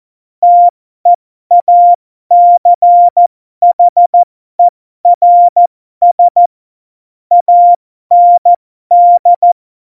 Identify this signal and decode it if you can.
Morse code, heard as a keyed tone, about 14 wpm, spaced as for 12 wpm, 700 Hz. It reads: TEACHERS AND
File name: Morsekurs / Witze